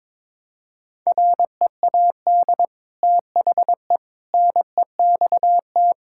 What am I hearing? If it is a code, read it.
Morse code, heard as a keyed tone, about 22 wpm, 700 Hz. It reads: READ THE NEXT
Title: Sentences from Top 1000 Words - Review Part 5 - 22wpm